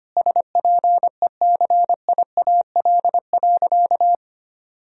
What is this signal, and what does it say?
Morse code, heard as a keyed tone, about 25 wpm, 700 Hz. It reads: SPECIAL.
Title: Sentences from Top 1000 Words - Review Part 3 - 25wpm